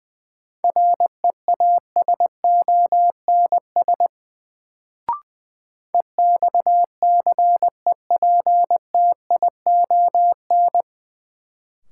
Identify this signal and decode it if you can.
Morse code, heard as a keyed tone, about 20 wpm, 700 Hz. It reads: REASONS EXCEPTION
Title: Sets of 4 Words - New Word Drill - 20wpm